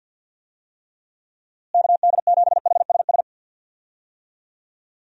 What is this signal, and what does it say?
Morse code, heard as a keyed tone, about 49 wpm, 700 Hz. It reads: KD7HSS
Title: USA Call Signs - 6 Characters - Repeated 2x - 50wpm